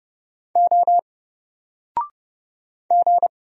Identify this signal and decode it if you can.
Morse code, heard as a keyed tone, about 30 wpm, 700 Hz. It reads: O G
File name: Single-Letter-Number-30wpm